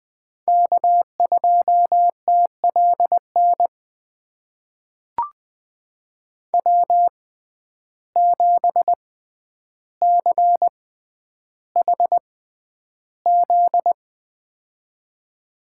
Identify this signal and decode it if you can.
Morse code, heard as a keyed tone, about 20 wpm, spaced as for 10 wpm, 700 Hz. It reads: K2TLN W7CHZ
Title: USA Call Signs - 5 Characters - 6x Character Spacing - 20wpm